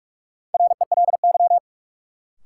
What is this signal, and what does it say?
Morse code, heard as a keyed tone, about 46 wpm, 700 Hz. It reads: RELY
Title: Sentences from Top 1200 Words - New Word Drill Alt - 45wpm